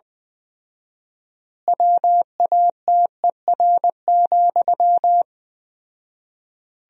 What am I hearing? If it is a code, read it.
Morse code, heard as a keyed tone, about 20 wpm, 700 Hz. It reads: WATER,
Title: 4x Word-Spacing - Sentences from Top 400 Words - 20wpm